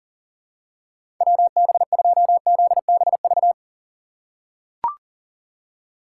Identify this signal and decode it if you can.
Morse code, heard as a keyed tone, about 40 wpm, 700 Hz. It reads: WB2ZBV
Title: USA Call Signs - 6 Characters - 2x Character Spacing - 40wpm